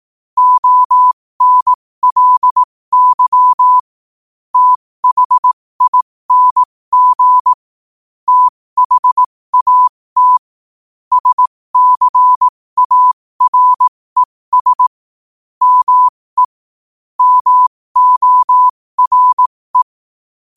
Morse code, heard as a keyed tone, about 18 wpm, spaced as for 15 wpm, 1000 Hz. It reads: ONLY THING THAT SCARES ME MORE
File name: qotdfiles / 15wpm